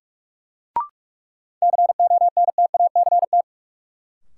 Morse code, heard as a keyed tone, about 44 wpm, 700 Hz. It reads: CONTACT
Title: Sentences from Top 1000 Words - Review Part 7 - New Word Drill - 45wpm